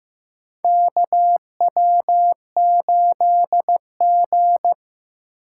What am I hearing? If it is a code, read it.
Morse code, heard as a keyed tone, about 15 wpm, 690 Hz. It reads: KW8G
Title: USA Call Signs - 4 Characters - 2x Character Spacing - 15wpm